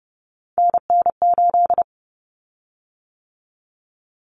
Morse code, heard as a keyed tone, about 30 wpm, 700 Hz. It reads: NN8
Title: Common Cut Numbers - 3-Digit 30wpm